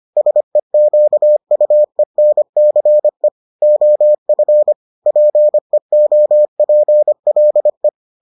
Morse code, heard as a keyed tone, about 25 wpm, 600 Hz. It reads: SEQUENCE OF PEOPLE